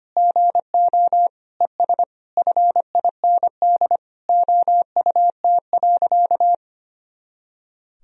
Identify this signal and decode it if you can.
Morse code, heard as a keyed tone, about 25 wpm, 700 Hz. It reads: GO ES FIND OUT.
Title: Sentences from Top 200 Words and Abbreviations 25wpm